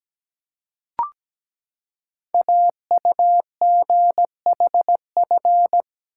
Morse code, heard as a keyed tone, about 17 wpm, 700 Hz. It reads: AUGHF
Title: Code Groups - 5 Letters - 17wpm